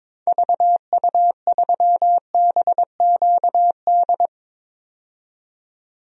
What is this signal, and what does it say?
Morse code, heard as a keyed tone, about 22 wpm, 700 Hz. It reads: VU3BQD
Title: DXCC-100 Call Signs - 6 Characters - 22wpm